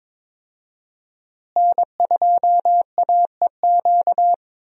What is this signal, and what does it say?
Morse code, heard as a keyed tone, about 21 wpm, 700 Hz. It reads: N2AEQ
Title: USA Call Signs - 5 Characters - Repeated 3x - 22wpm